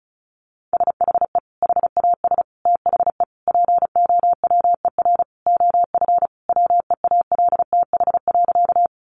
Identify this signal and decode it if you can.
Morse code, heard as a keyed tone, about 34 wpm, 700 Hz. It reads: SHE HAS THE POWER OF WEALTH.
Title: Sentences from Top 1200 Words 35wpm